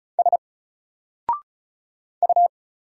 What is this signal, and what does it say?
Morse code, heard as a keyed tone, about 35 wpm, 700 Hz. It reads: S U
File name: Single-Letter-Number-35wpm